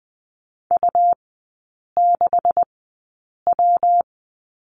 Morse code, heard as a keyed tone, about 20 wpm, spaced as for 8 wpm, 700 Hz. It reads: U6W
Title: 127 Lesson 21e - Number 6 - Three Characters 20wpm